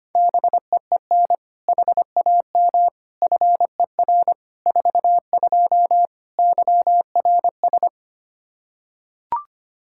Morse code, heard as a keyed tone, about 25 wpm, 700 Hz. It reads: BEEN HAM FER 42 YRS
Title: Practice QSO - Exchange 3 - More Info - 25wpm